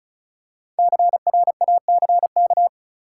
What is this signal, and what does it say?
Morse code, heard as a keyed tone, about 36 wpm, 700 Hz. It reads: CRACK
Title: Five Letter Words - 35wpm